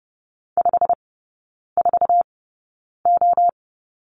Morse code, heard as a keyed tone, about 30 wpm, spaced as for 9 wpm, 700 Hz. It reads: 54O
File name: course / 30:10wpm / 087 Lesson 15e - Number 5 - Three Characters 30wpm